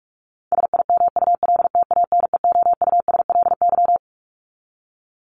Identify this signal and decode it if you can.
Morse code, heard as a keyed tone, about 44 wpm, 700 Hz. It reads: SIMULTANEOUSLY
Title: LY Suffix - Top 100 Words with Base Words 45wpm